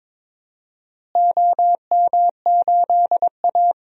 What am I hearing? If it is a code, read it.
Morse code, heard as a keyed tone, about 22 wpm, 700 Hz. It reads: OM8A
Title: DXCC-100 Call Signs - 4 Characters - Repeated 2x - 22wpm